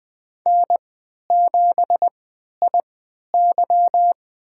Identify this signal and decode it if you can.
Morse code, heard as a keyed tone, about 20 wpm, spaced as for 11 wpm, 700 Hz. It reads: N7IY
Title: USA Call Signs - 4 Characters - 3x Character Spacing - 20wpm